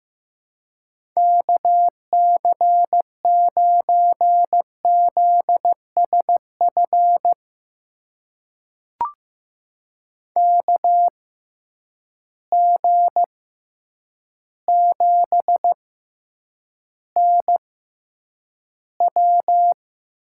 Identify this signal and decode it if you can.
Morse code, heard as a keyed tone, about 16 wpm, spaced as for 8 wpm, 700 Hz. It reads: KC9ZSF KG7NW
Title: USA Call Signs - 6x Character Spacing - 15wpm